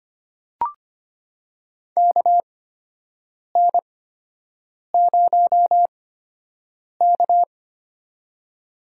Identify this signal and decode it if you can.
Morse code, heard as a keyed tone, about 25 wpm, spaced as for 7 wpm, 700 Hz. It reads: KN0K
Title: USA Call Signs - 4 Characters - 8x Character Spacing - 25wpm